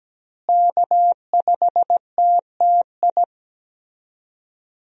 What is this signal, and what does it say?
Morse code, heard as a keyed tone, about 17 wpm, 700 Hz. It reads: K5TTI
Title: USA Call Signs - 5 Characters - Repeated 3x - 17wpm